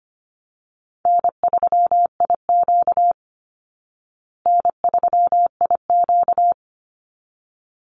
Morse code, heard as a keyed tone, about 25 wpm, 700 Hz. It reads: N3IQ N3IQ
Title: Call Signs Repeated 3x - 25wpm